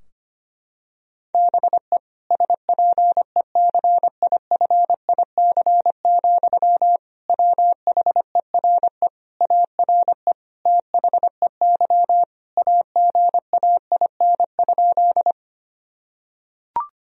Morse code, heard as a keyed tone, about 25 wpm, 700 Hz. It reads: BE SPECIFIC, WHERE ARE THEY AGAIN?
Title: Sentences from Top 1000 Words - Review Part 3 - 25wpm